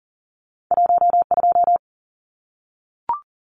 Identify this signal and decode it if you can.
Morse code, heard as a keyed tone, about 41 wpm, 700 Hz. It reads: 12
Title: Numbers - 2-Digit 40wpm